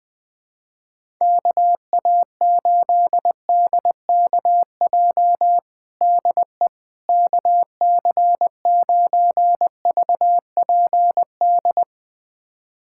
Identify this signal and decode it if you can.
Morse code, heard as a keyed tone, about 21 wpm, 700 Hz. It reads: KA8DKJ DE KC9VPD
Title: USA Callsign DE Callsign - 6 Characters - 20wpm